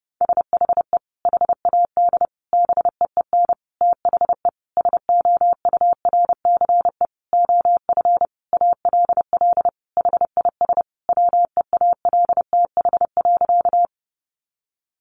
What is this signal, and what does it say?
Morse code, heard as a keyed tone, about 30 wpm, 700 Hz. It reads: SHE HAD BEEN THE SOURCE OF ALL HIS WEALTH.
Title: Sentences from Top 1200 Words 30wpm